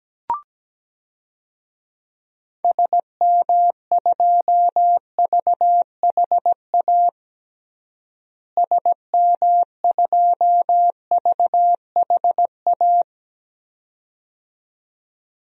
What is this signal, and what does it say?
Morse code, heard as a keyed tone, about 17 wpm, 700 Hz. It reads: SM2VHA SM2VHA
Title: DXCC-100 Call Signs - 6 Characters - Repeated 2x - 17wpm